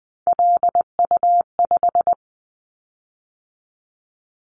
Morse code, heard as a keyed tone, about 20 wpm, 700 Hz. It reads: LU5
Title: Callsign Prefixes for Top 50 DXCC - 20wpm